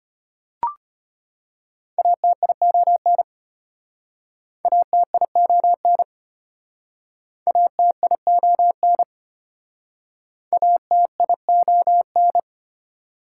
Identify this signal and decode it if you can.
Morse code, heard as a keyed tone, about 33 wpm, 690 Hz. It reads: ATION ATION ATION ATION